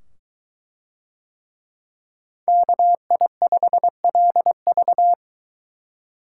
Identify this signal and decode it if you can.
Morse code, heard as a keyed tone, about 23 wpm, 700 Hz. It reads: KI5LV